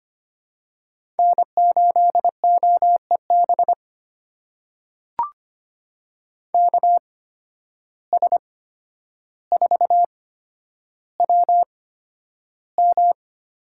Morse code, heard as a keyed tone, about 25 wpm, spaced as for 11 wpm, 700 Hz. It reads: N8OEB KS4WM
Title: USA Call Signs - 8x Character Spacing - 25wpm